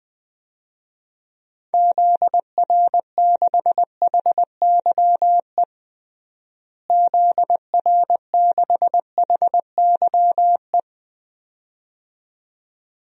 Morse code, heard as a keyed tone, about 21 wpm, 700 Hz. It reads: ZR6HYE ZR6HYE
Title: DXCC-100 Call Signs - 6 Characters - Repeated 2x - 20wpm